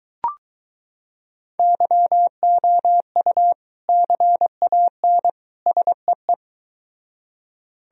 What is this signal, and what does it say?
Morse code, heard as a keyed tone, about 23 wpm, 700 Hz. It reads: YOU CAN SEE